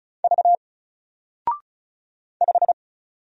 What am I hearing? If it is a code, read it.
Morse code, heard as a keyed tone, about 35 wpm, 700 Hz. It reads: V 5
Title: Single Letter-Number 35wpm